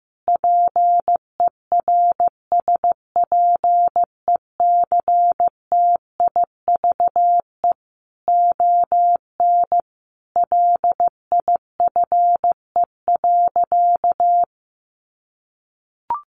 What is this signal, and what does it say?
Morse code, heard as a keyed tone, about 15 wpm, 700 Hz. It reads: PERSPECTIVE ON LIFE.
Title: Sentences from Top 1200 Words 15wpm